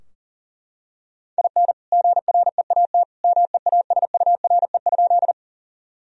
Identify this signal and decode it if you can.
Morse code, heard as a keyed tone, about 40 wpm, 700 Hz. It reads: IN GREAT MEASURE?
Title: Sentences from Top 1000 Words - Review Part 2 - 40wpm